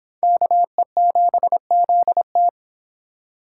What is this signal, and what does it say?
Morse code, heard as a keyed tone, about 30 wpm, 700 Hz. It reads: KE7ZT